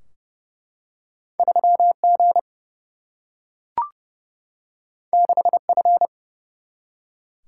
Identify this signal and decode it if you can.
Morse code, heard as a keyed tone, about 30 wpm, 700 Hz. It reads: 3G 6F
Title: Single-Letter Single-Number 30wpm